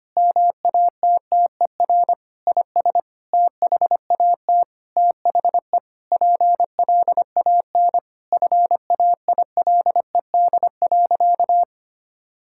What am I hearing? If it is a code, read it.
Morse code, heard as a keyed tone, about 25 wpm, 700 Hz. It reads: MATTER IS THAT THE PLAN FAILED.
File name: practice / 25wpm / Sentences with 9-Word Phrases - 25wpm